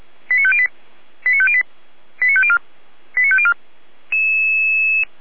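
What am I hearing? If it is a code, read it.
Five-tone selective call (ZVEI), heard as quick runs of five tones, tones of about 70 ms. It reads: 89508 89508 89504 89504
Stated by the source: ZVEI five-tone selective call 89150 received on SDR.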